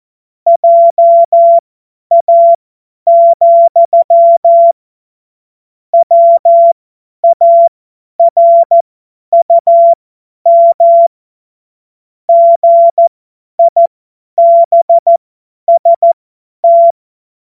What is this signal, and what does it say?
Morse code, heard as a keyed tone, about 14 wpm, spaced as for 10 wpm, 680 Hz. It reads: JA, WARUM GIBST
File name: Morsekurs / Witze / Witz 07. März 2023 50 F70